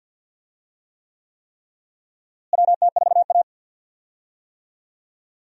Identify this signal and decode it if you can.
Morse code, heard as a keyed tone, about 48 wpm, 700 Hz. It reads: WT4A
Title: USA Call Signs - 4 Characters - Repeated 3x - 50wpm